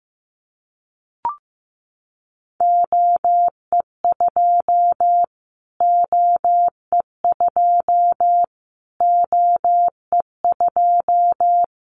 Morse code, heard as a keyed tone, about 15 wpm, 700 Hz. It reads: OE2 OE2 OE2
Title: Callsign Prefixes for Top 25 DXCC - Repeated 3x - 15wpm